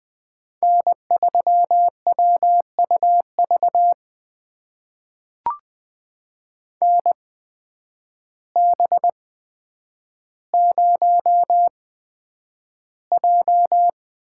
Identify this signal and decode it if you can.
Morse code, heard as a keyed tone, about 20 wpm, spaced as for 9 wpm, 700 Hz. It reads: N3WUV NB0J
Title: USA Call Signs - 8x Character Spacing - 20wpm